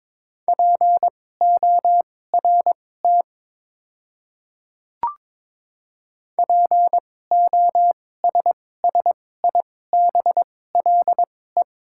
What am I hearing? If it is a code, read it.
Morse code, heard as a keyed tone, about 22 wpm, spaced as for 16 wpm, 690 Hz. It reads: PORT POSSIBLE
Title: 2x Character Spacing - Top 500 Words - 22wpm